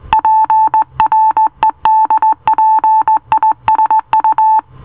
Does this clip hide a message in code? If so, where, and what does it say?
Yes: Morse code from 0.1 to 4.6 s, heard as a keyed tone, about 18 wpm, 880 Hz. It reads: PREDPISU